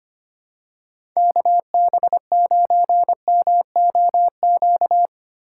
Morse code, heard as a keyed tone, about 25 wpm, 700 Hz. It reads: KB9MOQ
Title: USA Call Signs - 6 Characters - 6x Character Spacing - 25wpm